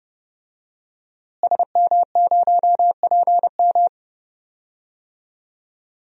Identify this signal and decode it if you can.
Morse code, heard as a keyed tone, about 31 wpm, 700 Hz. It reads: SM0PM
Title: DXCC-100 Call Signs - 5 Characters - Repeated 2x - 30wpm